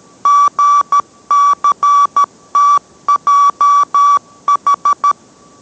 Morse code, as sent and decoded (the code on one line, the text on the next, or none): --. -.-. - .--- ....
GCTJH